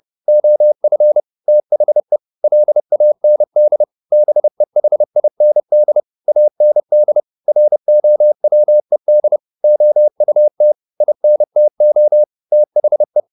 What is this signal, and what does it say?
Morse code, heard as a keyed tone, about 30 wpm, 600 Hz. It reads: OF THE LAND BEHIND AND ROWED OUT INTO THE
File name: the old man and the sea / the old man and the sea 1 30wpm